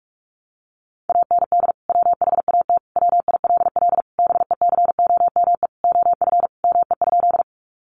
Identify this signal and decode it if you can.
Morse code, heard as a keyed tone, about 44 wpm, 700 Hz. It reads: AND WHAT WILL BECOME OF ME?
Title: Sentences from Top 600 Words 45wpm